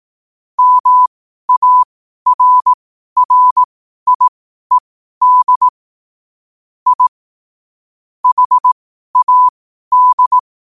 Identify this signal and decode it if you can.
Morse code, heard as a keyed tone, about 18 wpm, spaced as for 12 wpm, 1000 Hz. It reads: MARRIED I HAD
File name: qotdfiles / 12wpm